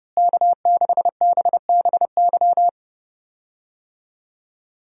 Morse code, heard as a keyed tone, about 30 wpm, 700 Hz. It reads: K6BBY
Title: USA Call Signs - 5 Characters - Repeated 2x - 30wpm